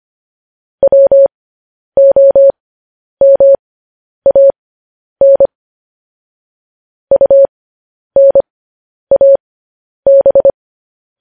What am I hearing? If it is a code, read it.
Morse code, heard as a keyed tone, about 25 wpm, spaced as for 10 wpm, 550 Hz. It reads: WOMAN UNAB
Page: news headlines in morse code at 10 wpm